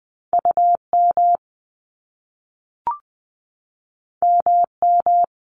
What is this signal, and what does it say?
Morse code, heard as a keyed tone, about 20 wpm, 700 Hz. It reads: UM MM
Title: Top Two Letter Combinations in Words 20wpm